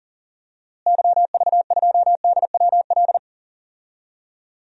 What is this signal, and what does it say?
Morse code, heard as a keyed tone, about 40 wpm, 700 Hz. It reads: YV2DWL
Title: DXCC-100 Call Signs - 6 Characters - Repeated 3x - 40wpm